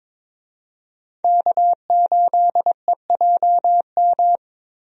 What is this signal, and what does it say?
Morse code, heard as a keyed tone, about 22 wpm, 700 Hz. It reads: K8EJM
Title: USA Call Signs - 5 Characters - Repeated 2x - 22wpm